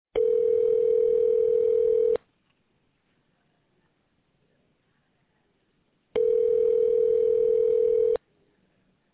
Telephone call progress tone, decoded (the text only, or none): ringback tone